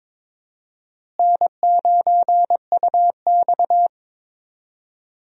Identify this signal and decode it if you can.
Morse code, heard as a keyed tone, about 22 wpm, 700 Hz. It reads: N9UX
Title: USA Call Signs - 4 Characters - Repeated 3x - 22wpm